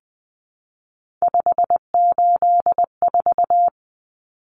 Morse code, heard as a keyed tone, about 20 wpm, 700 Hz. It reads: 584
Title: Numbers - 3-Digit 20wpm